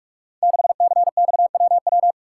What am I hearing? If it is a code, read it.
Morse code, heard as a keyed tone, about 44 wpm, 700 Hz. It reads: 6CXWW
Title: Code Groups - 5 Letter:Numbers - 45wpm